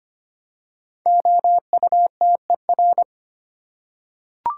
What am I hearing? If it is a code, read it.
Morse code, heard as a keyed tone, about 25 wpm, 700 Hz. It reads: OUTER
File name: 5 Letter Words - First 3 Letters Given - 25wpm